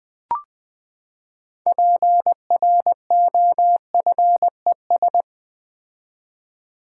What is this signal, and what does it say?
Morse code, heard as a keyed tone, about 20 wpm, 700 Hz. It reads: PROFES
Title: Build A Word - 6 Letters At A Time - 12-Letter Words - 20wpm